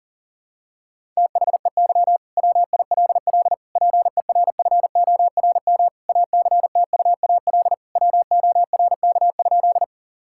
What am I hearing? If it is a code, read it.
Morse code, heard as a keyed tone, about 40 wpm, 700 Hz. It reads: THEY WILL PERFORM ACTUAL WORK?
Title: Sentences from Top 1000 Words - Review Part 5 - 40wpm